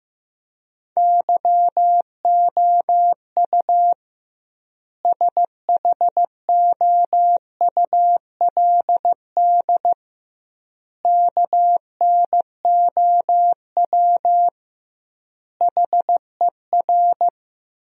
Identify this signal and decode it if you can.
Morse code, heard as a keyed tone, about 15 wpm, 700 Hz. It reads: YOU SHOULD KNOW HER
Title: 2x Word-Spacing - Sentences from Top 400 Words - 15wpm